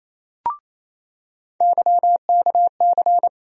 Morse code, heard as a keyed tone, about 27 wpm, 700 Hz. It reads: YKC
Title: USA Call Sign Postfixes - 3 Letters 28wpm